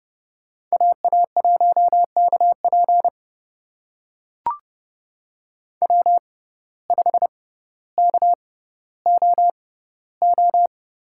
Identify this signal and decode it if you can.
Morse code, heard as a keyed tone, about 30 wpm, spaced as for 15 wpm, 700 Hz. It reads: AA1KP W5KOO
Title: USA Call Signs - 6x Character Spacing - 30wpm